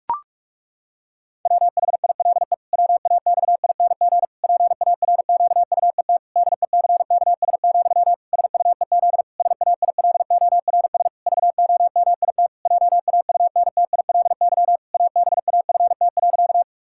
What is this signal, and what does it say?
Morse code, heard as a keyed tone, about 41 wpm, 700 Hz. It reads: WHILE WAXING PARQUET DECKS, SUEZ SAILORS VOMIT JAUNTILY ABAFT.
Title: Pangram Challenge v2 - 45wpm